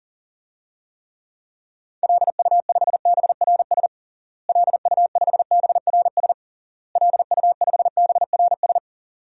Morse code, heard as a keyed tone, about 40 wpm, 700 Hz. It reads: LU5BRS LU5BRS LU5BRS
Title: DXCC-100 Call Signs - 6 Characters - Repeated 3x - 40wpm